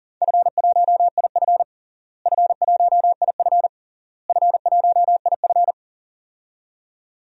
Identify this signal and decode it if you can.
Morse code, heard as a keyed tone, about 40 wpm, 700 Hz. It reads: F1IF F1IF F1IF
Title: DXCC-100 Call Signs - 4 Characters - Repeated 3x - 40wpm